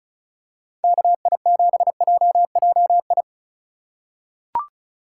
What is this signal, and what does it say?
Morse code, heard as a keyed tone, about 36 wpm, 700 Hz. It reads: KI7JJI
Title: USA Call Signs - 6 Characters - 4x Character Spacing - 35wpm